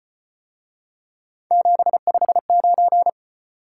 Morse code, heard as a keyed tone, about 34 wpm, 700 Hz. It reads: 759